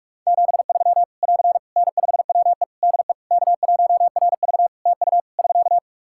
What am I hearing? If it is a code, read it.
Morse code, heard as a keyed tone, about 48 wpm, 700 Hz. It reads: 73 <AR> N5WE DE K1RV TU <SK>
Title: Practice QSO - Exchange 4 - Ending - 45wpm